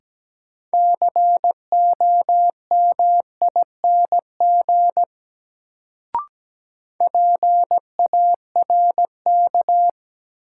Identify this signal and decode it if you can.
Morse code, heard as a keyed tone, about 17 wpm, 700 Hz. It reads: COMING PARK
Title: Sentences from Top 1300 Words - New Word Drill Alt - 17wpm